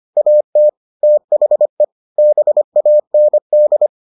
Morse code, heard as a keyed tone, about 25 wpm, 600 Hz. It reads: AT THE BAND